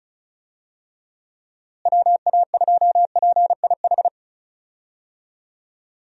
Morse code, heard as a keyed tone, about 34 wpm, 700 Hz. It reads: WA2PIH
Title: USA Call Signs - 6 Characters - Repeated 3x - 35wpm